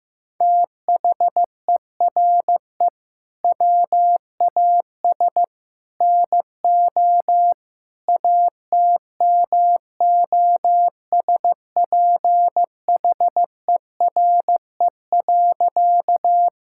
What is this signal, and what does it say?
Morse code, heard as a keyed tone, about 15 wpm, 700 Hz. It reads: THERE WAS NO ATMOSPHERE.
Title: Sentences from Top 1600 Words 15wpm